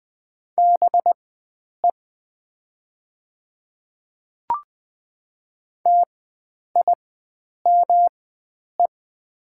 Morse code, heard as a keyed tone, about 20 wpm, spaced as for 9 wpm, 700 Hz. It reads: BE TIME
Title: 4x Character Spacing - Top 100 Words - 20wpm